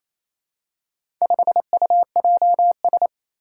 Morse code, heard as a keyed tone, about 28 wpm, 700 Hz. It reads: 5UJS
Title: Four Challenging Letter-Number Combinations 28wpm